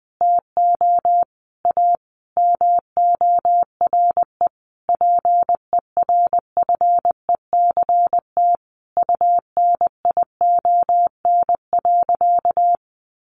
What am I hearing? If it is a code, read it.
Morse code, heard as a keyed tone, about 20 wpm, 700 Hz. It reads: TO A MORE PERFECT UNION.
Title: Sentences from Top 1400 Words 20wpm